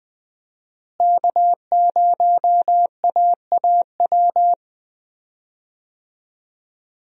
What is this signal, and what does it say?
Morse code, heard as a keyed tone, about 20 wpm, 700 Hz. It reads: K0AAW
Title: USA Call Signs - 5 Characters - Repeated 2x - 20wpm